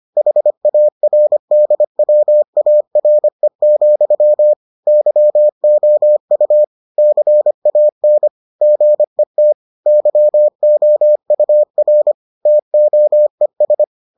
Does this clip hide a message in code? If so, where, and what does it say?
Yes: Morse code from 0.2 to 13.8 s, heard as a keyed tone, about 25 wpm, 600 Hz. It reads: HARDWARE, YOU CAN GET YOUR TOES